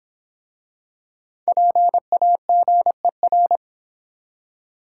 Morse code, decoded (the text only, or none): PAGER